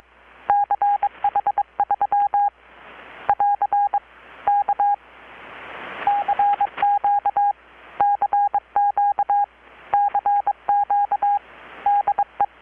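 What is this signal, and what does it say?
Morse code, heard as a keyed tone, about 22 wpm, 810 Hz. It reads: CH3 <AR> K CQ CQ CQ DE
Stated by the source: Morse from KPH's Night of Nights XX maritime radiotelegraphy event, received in Chiapas, Mexico about 3,800 km away on a Kenwood R-600 with a grounded 10 m long-wire antenna.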